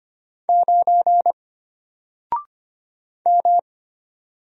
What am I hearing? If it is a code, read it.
Morse code, heard as a keyed tone, about 25 wpm, 700 Hz. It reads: 9 M
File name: Single-Letter-Number-25wpm